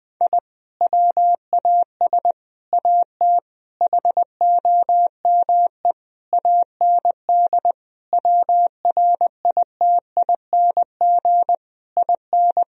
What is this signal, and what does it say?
Morse code, heard as a keyed tone, about 21 wpm, 700 Hz. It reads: I WAS AT HOME AND WRITING IN
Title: Sentences from Top 1000 Words - Review Part 1 - 20wpm